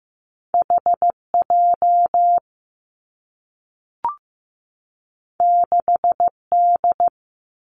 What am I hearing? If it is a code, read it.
Morse code, heard as a keyed tone, about 15 wpm, 700 Hz. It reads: H J 6 D
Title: Two Challenging Letter-Number Combinations 15wpm